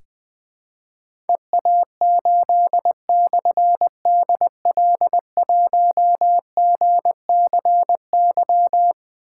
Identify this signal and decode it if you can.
Morse code, heard as a keyed tone, about 20 wpm, 700 Hz. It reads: EA8/DL1GCY
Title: Difficult Call Signs 20wpm